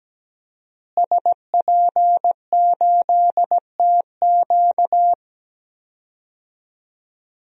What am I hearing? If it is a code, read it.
Morse code, heard as a keyed tone, about 17 wpm, 700 Hz. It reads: SP8TQ
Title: DXCC-100 Call Signs - 5 Characters - Repeated 2x - 17wpm